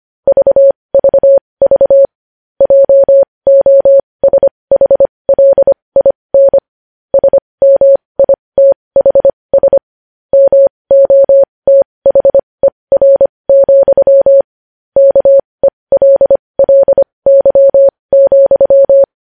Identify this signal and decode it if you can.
Morse code, heard as a keyed tone, about 25 wpm, spaced as for 20 wpm, 550 Hz. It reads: VVV JOSHLIN SMITHS MOTHER, KELLY,
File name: podcast / News headlines-1x-25x20